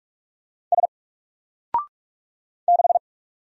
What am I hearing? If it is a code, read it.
Morse code, heard as a keyed tone, about 44 wpm, 700 Hz. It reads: S 6
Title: Single Letter-Number Alt 45wpm